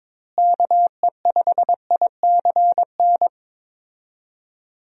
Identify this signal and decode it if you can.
Morse code, heard as a keyed tone, about 22 wpm, 700 Hz. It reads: KE5ICN
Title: USA Call Signs - 6 Characters - Repeated 2x - 22wpm